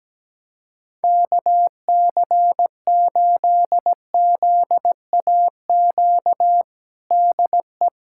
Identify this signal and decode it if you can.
Morse code, heard as a keyed tone, about 17 wpm, 700 Hz. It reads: KC8ZAQ DE
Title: USA Callsign DE Callsign 17wpm